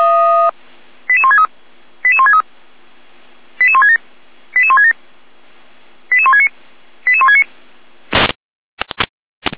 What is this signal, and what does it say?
Five-tone selective call (ZVEI), heard as quick runs of five tones, tones of about 70 ms. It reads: 88163 88163 88167 88167 88169 88169